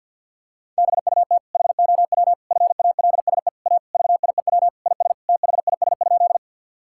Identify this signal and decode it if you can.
Morse code, heard as a keyed tone, about 50 wpm, 700 Hz. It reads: BUT HOW FALSE A VIEW IS THIS?
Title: Sentences from Top 1600 Words 50wpm